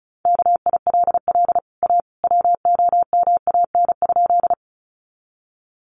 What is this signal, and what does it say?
Morse code, heard as a keyed tone, about 34 wpm, 700 Hz. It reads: KILL A WOMAN?
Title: Sentences from Top 1400 Words 35wpm